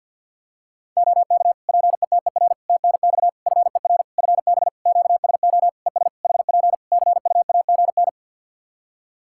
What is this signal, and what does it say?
Morse code, heard as a keyed tone, about 51 wpm, 700 Hz. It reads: OK PETER TNX FER FB QSO ES HP CUAGN
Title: K1USN SST QSO Practice 50wpm